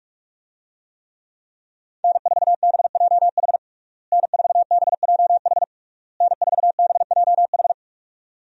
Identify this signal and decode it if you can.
Morse code, heard as a keyed tone, about 46 wpm, 700 Hz. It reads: N4BJH N4BJH N4BJH
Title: USA Call Signs - 5 Characters - Repeated 3x - 45wpm